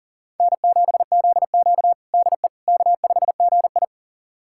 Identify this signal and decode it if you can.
Morse code, heard as a keyed tone, about 37 wpm, 700 Hz. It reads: N7ZQ DE K5GI
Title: USA Callsign DE Callsign - 4 Characters - 40wpm